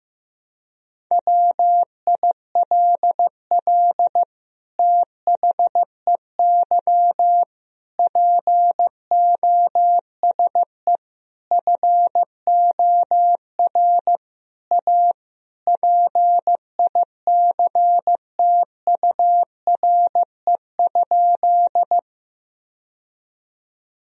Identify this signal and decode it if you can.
Morse code, heard as a keyed tone, about 15 wpm, 700 Hz. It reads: WILL THEY POSE FOR A PICTURE?
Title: Sentences from Top 1000 Words - Review Part 2 - 15wpm